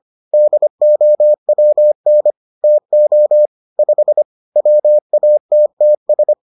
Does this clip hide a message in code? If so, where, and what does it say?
Yes: Morse code from 0.3 to 6.3 s, heard as a keyed tone, about 25 wpm, 600 Hz. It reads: DOWN TO 5 WATTS